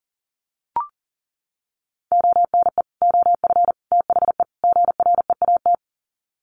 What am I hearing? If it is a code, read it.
Morse code, heard as a keyed tone, about 40 wpm, 700 Hz. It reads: ONE OF THE GREAT
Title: Sets of 4 Words 40wpm